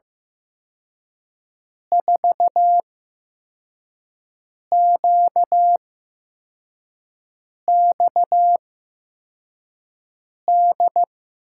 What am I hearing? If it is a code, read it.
Morse code, heard as a keyed tone, about 16 wpm, spaced as for 4 wpm, 700 Hz. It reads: 4QXD
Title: USA Call Signs - 8x Character Spacing - 15wpm